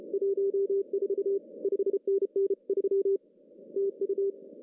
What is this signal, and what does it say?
Morse code, heard as a keyed tone, about 32 wpm, 390 Hz. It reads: 14 5NN 3 TU
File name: CW / cqww-cw-k5tr